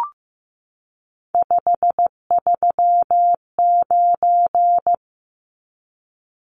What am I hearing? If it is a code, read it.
Morse code, heard as a keyed tone, about 15 wpm, 700 Hz. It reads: 539
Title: Numbers - 3-Digit 15wpm